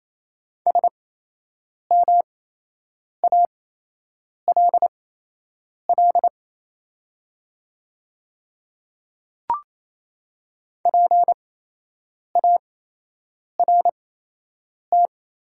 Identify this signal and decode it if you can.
Morse code, heard as a keyed tone, about 28 wpm, spaced as for 8 wpm, 700 Hz. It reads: SMALL PART
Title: 8x Character Spacing - Top 500 Words - 28wpm